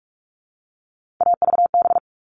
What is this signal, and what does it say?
Morse code, heard as a keyed tone, about 44 wpm, 700 Hz. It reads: AVB
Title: All Cut Numbers - 3-Digit 45wpm